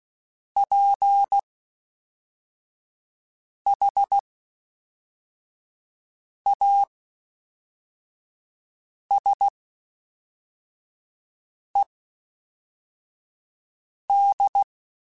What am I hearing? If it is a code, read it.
Morse code, heard as a keyed tone, about 16 wpm, spaced as for 4 wpm, 800 Hz. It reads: PHASED
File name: (spaced) / FromSMs spaced mp3s / Classes@5